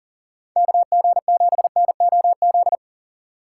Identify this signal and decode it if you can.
Morse code, heard as a keyed tone, about 40 wpm, 700 Hz. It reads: KG7NOZ
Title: USA Call Signs - 6 Characters - 4x Character Spacing - 40wpm